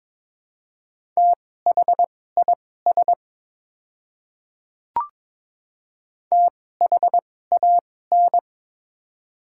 Morse code, heard as a keyed tone, about 22 wpm, spaced as for 16 wpm, 700 Hz. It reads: THIS THAN
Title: 2x Character Spacing - Top 100 Words - 22wpm